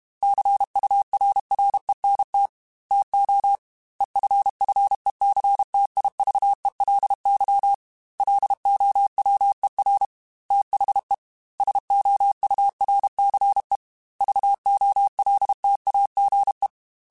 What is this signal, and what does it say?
Morse code, heard as a keyed tone, about 32 wpm, 800 Hz. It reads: CURRENT TO EFFECTIVELY LOWER THE SOURCE VOLTAGE